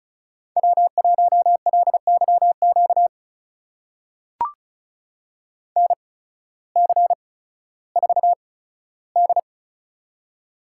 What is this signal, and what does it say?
Morse code, heard as a keyed tone, about 36 wpm, spaced as for 16 wpm, 700 Hz. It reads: W1LYQ NC4D
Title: USA Call Signs - 8x Character Spacing - 35wpm